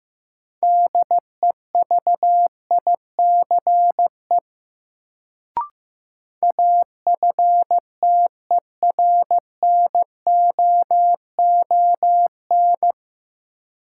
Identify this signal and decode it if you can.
Morse code, heard as a keyed tone, about 15 wpm, 700 Hz. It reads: DEVICE AFTERNOON
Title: Sentences from Top 1500 Words - New Word Drill Alt - 15wpm